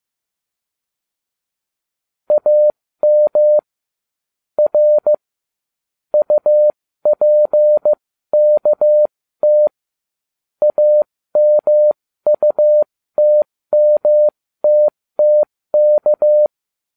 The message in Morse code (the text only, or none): AM R UPKT AMUTMTTK